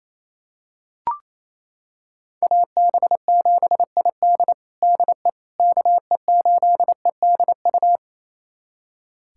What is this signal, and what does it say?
Morse code, heard as a keyed tone, about 27 wpm, 700 Hz. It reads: AB7ID DE KE8EDU
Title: USA Callsign DE Callsign 28wpm